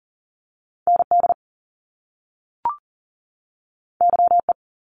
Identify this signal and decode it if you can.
Morse code, heard as a keyed tone, about 38 wpm, 700 Hz. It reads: ND YE
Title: Top Two Letter Combinations in Words 40wpm